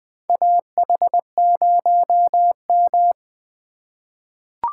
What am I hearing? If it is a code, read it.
Morse code, heard as a keyed tone, about 20 wpm, 700 Hz. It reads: AH0M
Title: USA Call Signs - 4 Characters - 2x Character Spacing - 20wpm